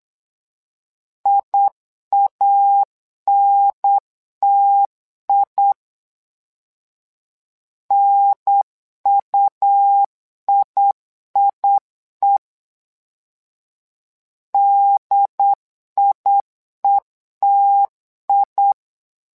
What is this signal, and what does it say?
Morse code, heard as a keyed tone, about 8 wpm, 800 Hz. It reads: IANTI NUIIE DIETI
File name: LEZIONI / 2 UHDN / 2 UHDN 8wpm2232